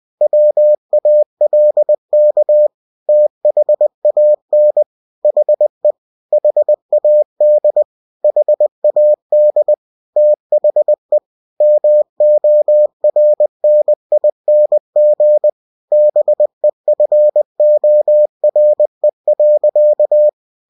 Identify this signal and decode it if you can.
Morse code, heard as a keyed tone, about 20 wpm, 600 Hz. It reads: WALK THAN HE HAD HAD THE MORNING BEFORE.
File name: cw-books / winnie-the-pooh / winnie-20wpm-page-0096